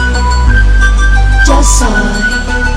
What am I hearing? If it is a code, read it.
Speech. There is no code.